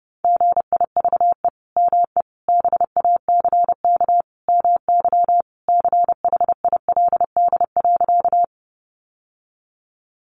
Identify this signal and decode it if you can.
Morse code, heard as a keyed tone, about 30 wpm, 700 Hz. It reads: GIVE ME BACK MY CHILD.
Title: Sentences from Top 800 Words 30wpm